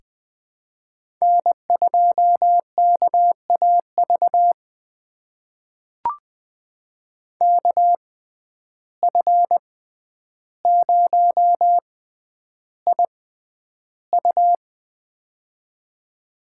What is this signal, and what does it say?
Morse code, heard as a keyed tone, about 20 wpm, spaced as for 10 wpm, 700 Hz. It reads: N2KAV KF0IU
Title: USA Call Signs - 5 Characters - 6x Character Spacing - 20wpm